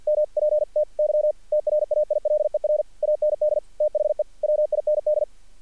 Morse code, heard as a keyed tone, about 48 wpm, 600 Hz. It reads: MPTY TRAILER AND THE WIND
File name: archived / CRSnet